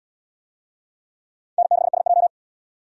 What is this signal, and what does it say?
Morse code, heard as a keyed tone, about 78 wpm, 700 Hz. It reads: N6SJ